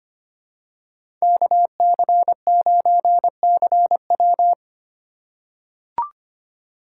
Morse code, heard as a keyed tone, about 25 wpm, 700 Hz. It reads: KC9CW
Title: USA Call Signs - 5 Characters - 4x Character Spacing - 25wpm